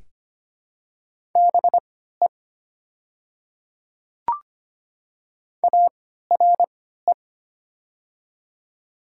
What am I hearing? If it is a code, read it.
Morse code, heard as a keyed tone, about 25 wpm, spaced as for 14 wpm, 700 Hz. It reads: BE ARE
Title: 3x Character Spacing - Top 100 Words - 25wpm